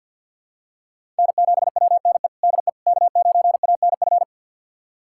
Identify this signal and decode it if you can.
Morse code, heard as a keyed tone, about 52 wpm, 700 Hz. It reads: N7WNE DE K9ANF
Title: USA Callsign DE Callsign - 5 Characters - 50wpm